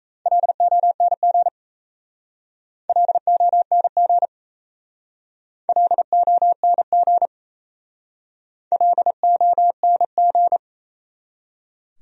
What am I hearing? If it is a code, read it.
Morse code, heard as a keyed tone, about 37 wpm, 700 Hz. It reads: LONG LONG LONG LONG